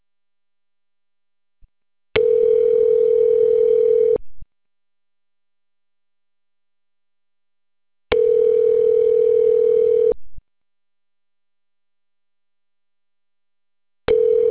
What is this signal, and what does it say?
Ringback tone (North America), 440 + 480 Hz.